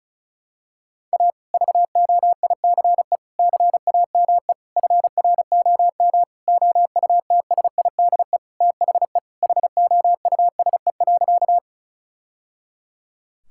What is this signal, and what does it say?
Morse code, heard as a keyed tone, about 36 wpm, 700 Hz. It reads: A VOICE CAME FROM OUTSIDE THE HOUSE.
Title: Sentences from Top 1000 Words - Review Part 6 - 35wpm